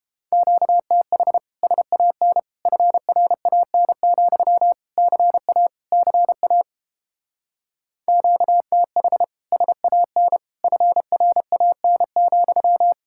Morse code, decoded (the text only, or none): QTH SAN FRAN, CA CA QTH SAN FRAN,